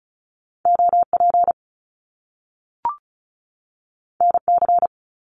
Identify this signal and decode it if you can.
Morse code, heard as a keyed tone, about 36 wpm, 700 Hz. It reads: OP NC
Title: Top Two Letter Combinations in Words 35wpm